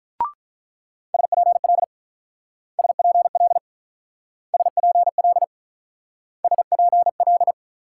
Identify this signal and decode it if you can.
Morse code, heard as a keyed tone, about 43 wpm, 700 Hz. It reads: SPL SPL SPL SPL
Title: Speed-Racing Multi-letter Phonemes - 35wpm